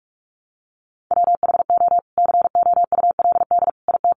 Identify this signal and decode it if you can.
Morse code, heard as a keyed tone, about 46 wpm, 700 Hz. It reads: WHO COULD IT
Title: Sentences from Top 1300 Words 45wpm